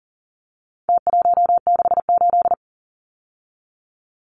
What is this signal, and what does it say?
Morse code, heard as a keyed tone, about 40 wpm, 700 Hz. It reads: T168
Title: Numbers - Contest Style - 4-Digit 40wpm